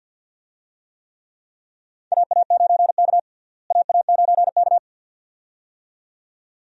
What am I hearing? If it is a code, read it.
Morse code, heard as a keyed tone, about 51 wpm, 700 Hz. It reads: AA9K AA9K
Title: USA Call Signs - 4 Characters - Repeated 2x - 50wpm